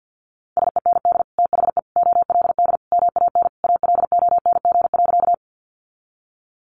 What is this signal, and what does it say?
Morse code, heard as a keyed tone, about 50 wpm, 700 Hz. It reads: SEND THE OLD MAN ALONG.
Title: Sentences from Top 1500 Words 50wpm